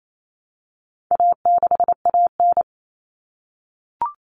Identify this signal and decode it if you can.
Morse code, heard as a keyed tone, about 28 wpm, 700 Hz. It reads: A6AN
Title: Common Cut Numbers - 4-Digit 28wpm